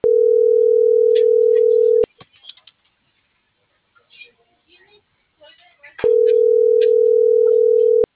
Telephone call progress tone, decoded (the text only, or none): ringback tone